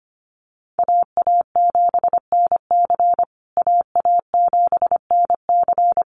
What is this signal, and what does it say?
Morse code, heard as a keyed tone, about 25 wpm, 700 Hz. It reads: AA7NC AA7NC
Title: USA Call Signs Repeated 3x 25wpm